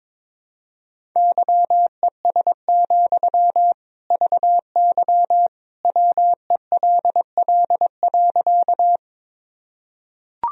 Morse code, heard as a keyed tone, about 22 wpm, 700 Hz. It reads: YES, VY WELL.
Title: Sentences from Top 500 Words and Abbreviations 22wpm